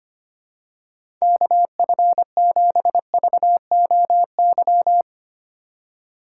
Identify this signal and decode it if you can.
Morse code, heard as a keyed tone, about 25 wpm, 700 Hz. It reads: KF7VOY
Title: USA Call Signs - 4x Character Spacing - 25wpm